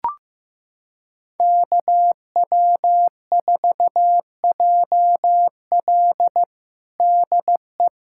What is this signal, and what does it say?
Morse code, heard as a keyed tone, about 15 wpm, 700 Hz. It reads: KW4JL DE
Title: USA Callsign DE Callsign - 5 Characters - 15wpm